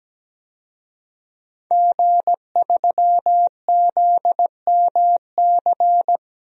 Morse code, heard as a keyed tone, about 17 wpm, 700 Hz. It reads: G3ZMC
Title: DXCC-100 Call Signs - 5 Characters - 17wpm